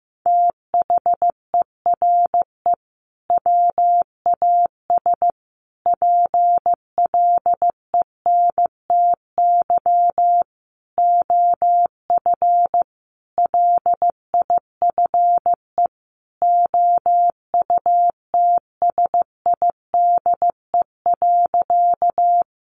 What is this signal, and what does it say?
Morse code, heard as a keyed tone, about 15 wpm, 700 Hz. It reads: THERE WAS PLENTY OF LIFE OUTSIDE.
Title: Sentences from Top 1300 Words 15wpm